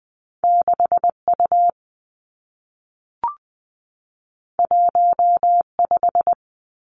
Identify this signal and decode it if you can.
Morse code, heard as a keyed tone, about 20 wpm, 700 Hz. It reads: 6 U 1 5
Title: Two Challenging Letter-Number Combinations 20wpm